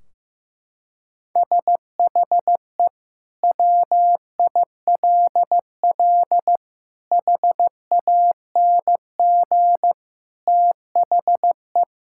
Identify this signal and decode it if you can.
Morse code, heard as a keyed tone, about 16 wpm, 700 Hz. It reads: SHE WILL HANG THE
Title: Sentences from Top 1600 Words 15wpm